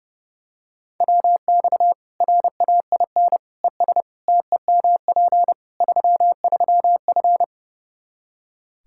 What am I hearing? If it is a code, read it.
Morse code, heard as a keyed tone, about 32 wpm, 700 Hz. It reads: WX RAIN ES TEMP 33F
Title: Practice QSO - Exchange 2 - Info - 30wpm